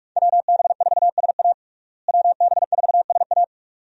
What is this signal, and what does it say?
Morse code, heard as a keyed tone, about 42 wpm, 700 Hz. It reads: WB4SA WB4SA
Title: USA Call Signs - 5 Characters - Repeated 2x - 45wpm